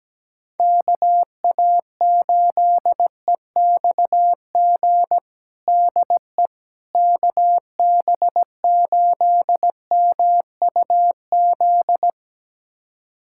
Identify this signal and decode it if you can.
Morse code, heard as a keyed tone, about 17 wpm, 700 Hz. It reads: KA8EXG DE KB8MUZ
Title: USA Callsign DE Callsign 17wpm